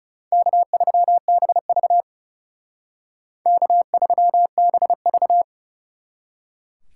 Morse code, heard as a keyed tone, about 37 wpm, 700 Hz. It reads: K3BV K3BV
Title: Speed Racing - USA 1x2 Call Signs 30wpm